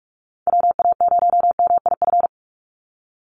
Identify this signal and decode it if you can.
Morse code, heard as a keyed tone, about 44 wpm, 700 Hz. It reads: WA0MIF
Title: USA Call Signs 45wpm